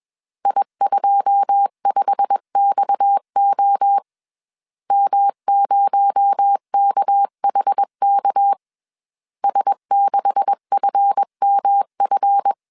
Morse code, heard as a keyed tone, about 21 wpm, 800 Hz. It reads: I25XO M0KHK S6FMF
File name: CW / DET / 1 9 1500C S C 20wpm 2310